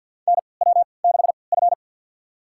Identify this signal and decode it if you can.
Morse code, heard as a keyed tone, about 49 wpm, spaced as for 28 wpm, 700 Hz. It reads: NW6F
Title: USA Call Signs - 4 Characters - 3x Character Spacing - 50wpm